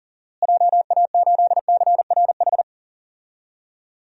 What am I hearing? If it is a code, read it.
Morse code, heard as a keyed tone, about 40 wpm, 700 Hz. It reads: JA8CRH
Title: DXCC-100 Call Signs - 6 Characters - Repeated 3x - 40wpm